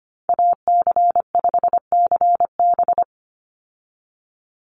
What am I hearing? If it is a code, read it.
Morse code, heard as a keyed tone, about 25 wpm, 700 Hz. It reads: AC5CB
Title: USA Call Signs 25wpm